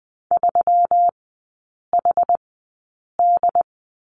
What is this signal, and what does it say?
Morse code, heard as a keyed tone, about 20 wpm, spaced as for 8 wpm, 700 Hz. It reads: 3HD
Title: 121 Lesson 20e - Number 3 - Three Characters 20wpm